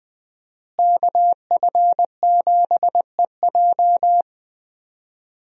Morse code, heard as a keyed tone, about 19 wpm, 700 Hz. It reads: KF7EJ
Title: USA Call Signs - 5 Characters - Repeated 3x - 20wpm